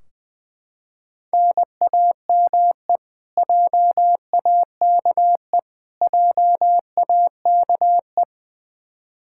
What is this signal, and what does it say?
Morse code, heard as a keyed tone, about 21 wpm, 700 Hz. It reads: NAME JAKE JAKE
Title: Practice QSO - Exchange 1 - Beginning - 20wpm